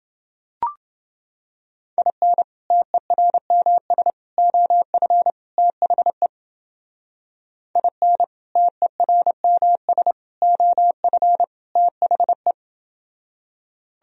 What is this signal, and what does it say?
Morse code, heard as a keyed tone, about 30 wpm, 700 Hz. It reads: IN TERMS OF THE IN TERMS OF THE